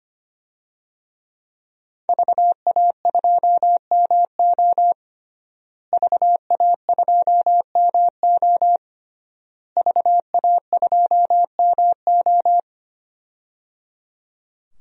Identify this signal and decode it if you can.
Morse code, heard as a keyed tone, about 25 wpm, 700 Hz. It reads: VA2MO VA2MO VA2MO
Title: DXCC-100 Call Signs - 5 Characters - Repeated 3x - 25wpm